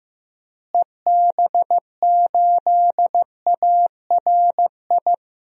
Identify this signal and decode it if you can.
Morse code, heard as a keyed tone, about 15 wpm, 700 Hz. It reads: EB8ARI
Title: DXCC-100 Call Signs - 6 Characters - Repeated 3x - 15wpm